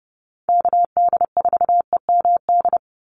Morse code, heard as a keyed tone, about 30 wpm, 700 Hz. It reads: KD4EMD
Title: USA Call Signs 30wpm